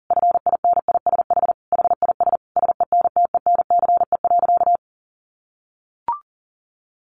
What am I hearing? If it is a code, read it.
Morse code, heard as a keyed tone, about 40 wpm, 700 Hz. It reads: FINISH HIS SENTENCE.
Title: Sentences from Top 1200 Words 40wpm